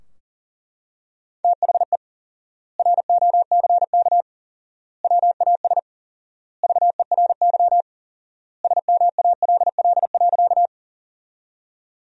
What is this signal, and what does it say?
Morse code, heard as a keyed tone, about 40 wpm, 700 Hz. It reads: THE ROCK WAS VERY SMALL.
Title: 4x Word-Spacing - Sentences from Top 400 Words - 40wpm